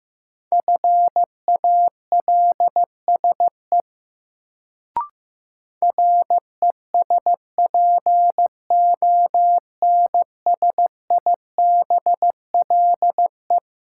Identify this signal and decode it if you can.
Morse code, heard as a keyed tone, about 15 wpm, 700 Hz. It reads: FALSE RESPONSIBLE
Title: Sentences from Top 1600 Words - New Word Drill Alt - 15wpm